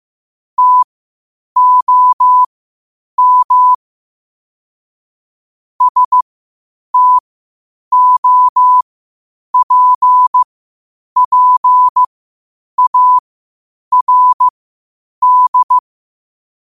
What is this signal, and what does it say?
Morse code, heard as a keyed tone, about 15 wpm, spaced as for 8 wpm, 1000 Hz. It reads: TOM STOPPARD
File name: qotdfiles / 8wpm